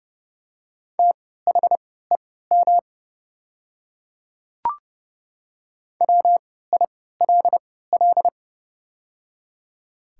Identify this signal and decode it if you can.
Morse code, heard as a keyed tone, about 30 wpm, spaced as for 17 wpm, 700 Hz. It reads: THEM WILL
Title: 3x Character Spacing - Top 100 Words - 30wpm